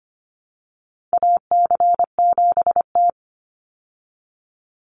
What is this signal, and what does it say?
Morse code, heard as a keyed tone, about 25 wpm, 700 Hz. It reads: AC7T
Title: Call Signs Repeated 2x - 25wpm